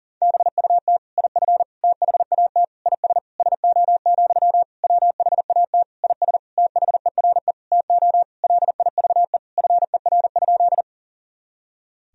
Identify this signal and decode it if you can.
Morse code, heard as a keyed tone, about 40 wpm, 700 Hz. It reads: BUT IF THAT IS SO, WHAT IS THERE TO LIVE FER?
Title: Sentences from Top 200 Words and Abbreviations 40wpm